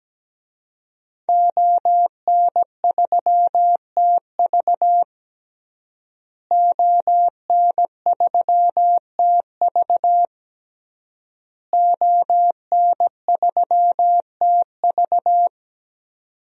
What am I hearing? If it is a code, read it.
Morse code, heard as a keyed tone, about 17 wpm, 700 Hz. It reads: ON3TV ON3TV ON3TV
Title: DXCC-100 Call Signs - 5 Characters - Repeated 3x - 17wpm